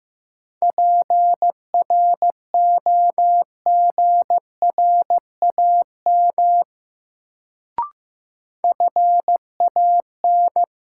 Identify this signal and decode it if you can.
Morse code, heard as a keyed tone, about 15 wpm, 700 Hz. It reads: PROGRAM FAN
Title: Sentences from Top 1000 Words - Review Part 4 - New Word Drill - 15wpm